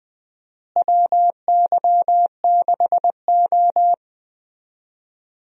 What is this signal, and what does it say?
Morse code, heard as a keyed tone, about 21 wpm, 700 Hz. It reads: WY6O
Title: USA Call Signs - 4 Characters - Repeated 2x - 20wpm